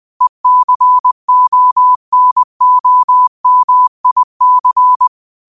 Morse code, heard as a keyed tone, about 20 wpm, 1000 Hz. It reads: ECONOMIC
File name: qotdfiles / 20wpm